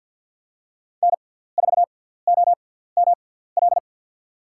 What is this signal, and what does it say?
Morse code, heard as a keyed tone, about 50 wpm, spaced as for 17 wpm, 700 Hz. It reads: N4OML
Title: USA Call Signs - 5 Characters - 6x Character Spacing - 50wpm